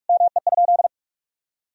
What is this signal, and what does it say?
Morse code, heard as a keyed tone, about 41 wpm, 700 Hz. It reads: ME?
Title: Sentences from Top 1000 Words - Review Part 4 - 45wpm